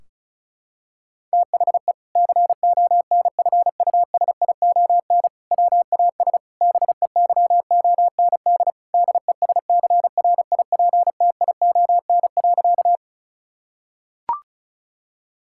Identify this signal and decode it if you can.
Morse code, heard as a keyed tone, about 34 wpm, 700 Hz. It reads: THE CONFUSION WAS BEYOND DESCRIPTION.
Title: Sentences from Top 1600 Words 35wpm